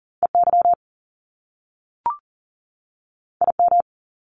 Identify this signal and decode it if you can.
Morse code, heard as a keyed tone, about 41 wpm, 700 Hz. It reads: EY IM
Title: Top Two Letter Combinations in Words 40wpm